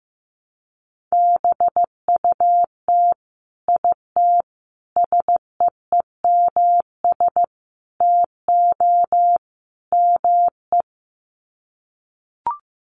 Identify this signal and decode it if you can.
Morse code, heard as a keyed tone, about 15 wpm, 700 Hz. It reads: BUT IT SEEMS TO ME
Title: Sets of 5 Words Encore2 15wpm